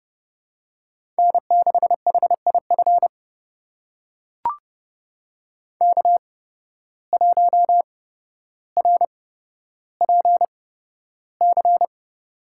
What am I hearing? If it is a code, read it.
Morse code, heard as a keyed tone, about 31 wpm, spaced as for 13 wpm, 700 Hz. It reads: N6HIF K1RPC
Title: USA Call Signs - 5 Characters - 8x Character Spacing - 30wpm